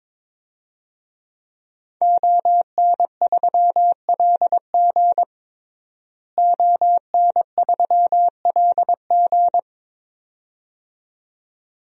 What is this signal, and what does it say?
Morse code, heard as a keyed tone, about 22 wpm, 700 Hz. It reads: ON3LG ON3LG
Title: DXCC-100 Call Signs - 5 Characters - Repeated 2x - 22wpm